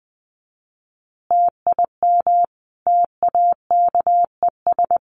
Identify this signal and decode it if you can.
Morse code, heard as a keyed tone, about 20 wpm, 700 Hz. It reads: TIM TAKES
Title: Easy Phrases 20wpm